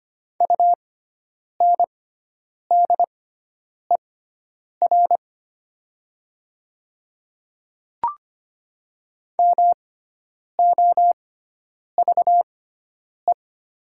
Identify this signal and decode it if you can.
Morse code, heard as a keyed tone, about 25 wpm, spaced as for 9 wpm, 700 Hz. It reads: UNDER MOVE
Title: 6x Character Spacing - Top 300 Words - 25wpm